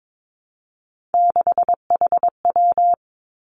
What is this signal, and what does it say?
Morse code, heard as a keyed tone, about 22 wpm, 700 Hz. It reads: 6HW
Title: Three Challenging Letter-Number Combinations 22wpm